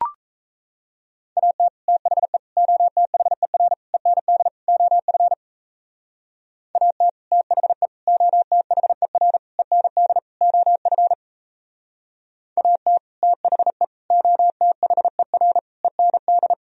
Morse code, heard as a keyed tone, about 34 wpm, 700 Hz. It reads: AT THE OTHER END OF AT THE OTHER END OF AT THE OTHER END
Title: Speed-Racing - Sets of 5 Words 28wpm